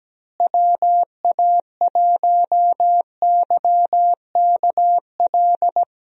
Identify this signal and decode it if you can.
Morse code, heard as a keyed tone, about 17 wpm, 700 Hz. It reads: WA1YKL
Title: USA Call Signs - 6 Characters - Repeated 2x - 17wpm